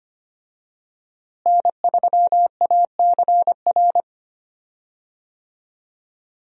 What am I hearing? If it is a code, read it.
Morse code, heard as a keyed tone, about 25 wpm, 700 Hz. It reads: N3ACR
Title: USA Call Signs - 5 Characters - Repeated 3x - 25wpm